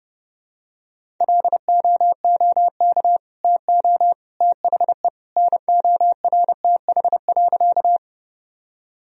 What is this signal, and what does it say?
Morse code, heard as a keyed tone, about 32 wpm, 700 Hz. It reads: LOOK TO THE NORTH.
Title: Sentences from Top 1000 Words - Review Part 1 - 30wpm